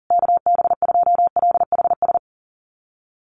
Morse code, heard as a keyed tone, about 40 wpm, 700 Hz. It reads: KB2LHS
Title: USA Call Signs Repeated 3x 40wpm